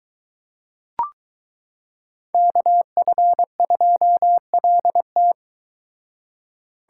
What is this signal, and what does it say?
Morse code, heard as a keyed tone, about 22 wpm, 700 Hz. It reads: KF2LT